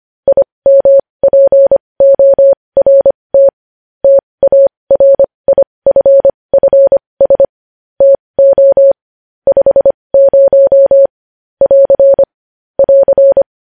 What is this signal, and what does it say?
Morse code, heard as a keyed tone, about 25 wpm, spaced as for 20 wpm, 550 Hz. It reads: IMPORT TARIFFS TO 50 <AR> <AR>
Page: news-headlines-in-morse-code-at-20-wpm